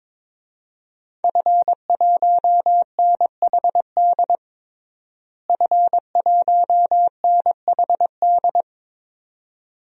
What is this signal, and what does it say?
Morse code, heard as a keyed tone, about 21 wpm, 700 Hz. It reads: F1NHD F1NHD
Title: DXCC-100 Call Signs - 5 Characters - Repeated 2x - 22wpm